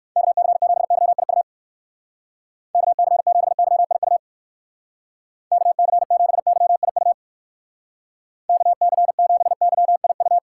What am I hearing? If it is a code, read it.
Morse code, heard as a keyed tone, about 54 wpm, 700 Hz. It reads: KC7YIU KC7YIU KC7YIU KC7YIU
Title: Speed Racing - USA 2x3 Call Signs 45wpm